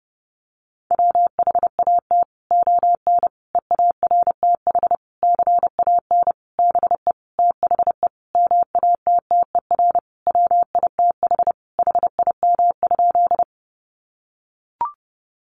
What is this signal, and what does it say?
Morse code, heard as a keyed tone, about 30 wpm, 700 Hz. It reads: WHAT ON EARTH CAN BE THE MATTER WITH HIM?
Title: Sentences from Top 700 Words 30wpm